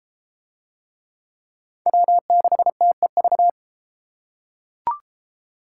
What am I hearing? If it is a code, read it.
Morse code, heard as a keyed tone, about 29 wpm, 700 Hz. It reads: W6TEV